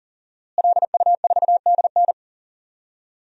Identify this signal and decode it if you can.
Morse code, heard as a keyed tone, about 40 wpm, 700 Hz. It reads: LU4DN
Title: DXCC-100 Call Signs - 5 Characters - Repeated 3x - 40wpm